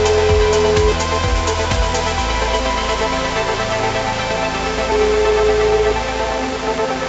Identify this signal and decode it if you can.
Ringback tone (Europe, 425 Hz), with noise.